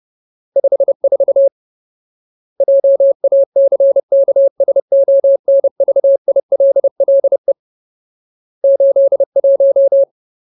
Morse code, heard as a keyed tone, about 30 wpm, 550 Hz. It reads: 54 JACKSONVILLE 81
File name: Ultimate Word List / 30WPM Ultimate